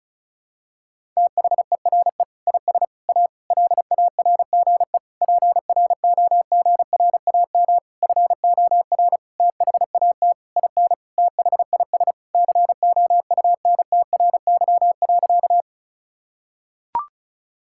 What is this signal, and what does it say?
Morse code, heard as a keyed tone, about 36 wpm, 700 Hz. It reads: THERE IS A LARGE PROGRAM FOR THAT IN THIS COUNTRY.
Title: Sentences from Top 1000 Words - Review Part 5 - 35wpm